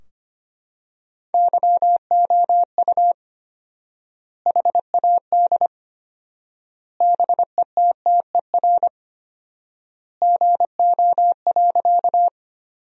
Morse code, heard as a keyed tone, about 25 wpm, 700 Hz. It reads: YOU HAD BETTER GO.
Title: 4x Word-Spacing - Sentences from Top 400 Words - 25wpm